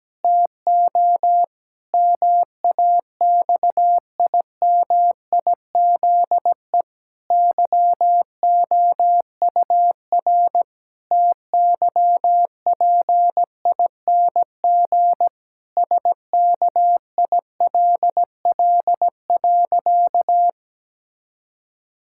Morse code, heard as a keyed tone, about 17 wpm, 700 Hz. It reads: TO MAXIMIZE YOUR TYPING SKILL.
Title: Pangram Challenge v2 - 17wpm